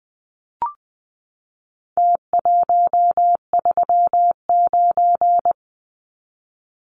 Morse code, heard as a keyed tone, about 20 wpm, 700 Hz. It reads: T139
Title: Numbers - Contest Style - 4-Digit 20wpm